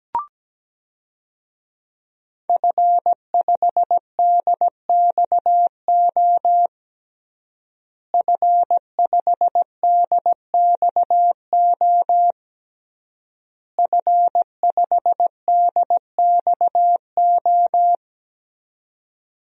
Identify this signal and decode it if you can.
Morse code, heard as a keyed tone, about 17 wpm, 700 Hz. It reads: F5DXO F5DXO F5DXO
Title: DXCC-100 Call Signs - 5 Characters - Repeated 3x - 17wpm